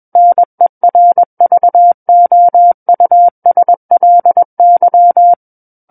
Morse code, heard as a keyed tone, about 21 wpm, 700 Hz. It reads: NERVOUSLY
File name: Book One / 12-Chapter 12 - 21 WPM